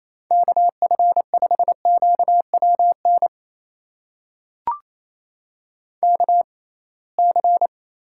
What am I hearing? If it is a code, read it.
Morse code, heard as a keyed tone, about 28 wpm, spaced as for 21 wpm, 700 Hz. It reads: KF5QWN KC
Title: USA Call Signs - 6 Characters - 6x Character Spacing - 28wpm